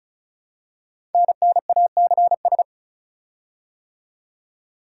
Morse code, heard as a keyed tone, about 36 wpm, 700 Hz. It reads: NNACS
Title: Code Groups - 5 Letters - 35wpm